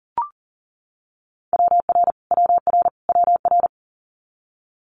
Morse code, heard as a keyed tone, about 39 wpm, 700 Hz. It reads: WR WR WR
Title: Multi-letter Phonemes Repeated 3x 40wpm